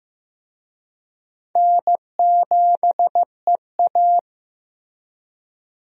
Morse code, heard as a keyed tone, about 15 wpm, 700 Hz. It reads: N7EA
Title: USA Call Signs - 4 Characters - Repeated 2x - 15wpm